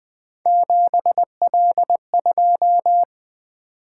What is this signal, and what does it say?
Morse code, heard as a keyed tone, about 20 wpm, 700 Hz. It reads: 7L2
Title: Three Letter-Number Combinations - 20wpm